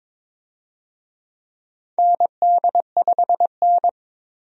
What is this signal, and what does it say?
Morse code, heard as a keyed tone, about 21 wpm, 700 Hz. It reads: ND5N
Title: USA Call Signs - 4 Characters - 22wpm